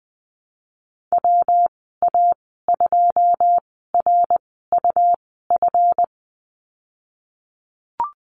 Morse code, heard as a keyed tone, about 20 wpm, spaced as for 15 wpm, 700 Hz. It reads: WA2RUF
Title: 173 Lesson 28g - Number 9 - Call Signs 20wpm